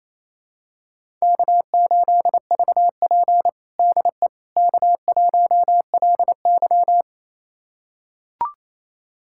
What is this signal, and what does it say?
Morse code, heard as a keyed tone, about 28 wpm, 700 Hz. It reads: K8VP DE K1LY
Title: USA Callsign DE Callsign - 4 Characters - 28wpm